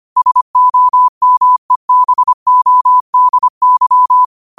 Morse code, heard as a keyed tone, about 25 wpm, 1000 Hz. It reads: SOMEBODY
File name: qotdfiles / 25wpm